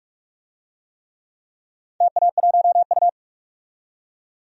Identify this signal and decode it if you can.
Morse code, heard as a keyed tone, about 44 wpm, 700 Hz. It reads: TA1U
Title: DXCC-100 Call Signs - 4 Characters - 45wpm